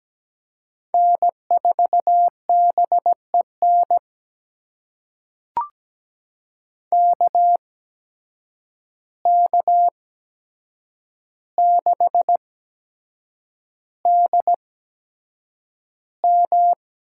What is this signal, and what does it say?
Morse code, heard as a keyed tone, about 17 wpm, spaced as for 7 wpm, 700 Hz. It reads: N4BEN KK6DM
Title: USA Call Signs - 5 Characters - 8x Character Spacing - 17wpm